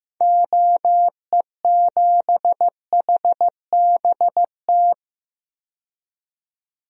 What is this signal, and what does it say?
Morse code, heard as a keyed tone, about 15 wpm, 700 Hz. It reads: OE7HBT
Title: DXCC-100 Call Signs - 6 Characters - 15wpm